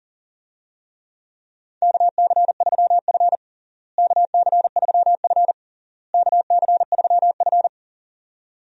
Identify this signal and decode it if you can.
Morse code, heard as a keyed tone, about 40 wpm, 700 Hz. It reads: KC3F KC3F KC3F
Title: USA Call Signs - 4 Characters - Repeated 3x - 40wpm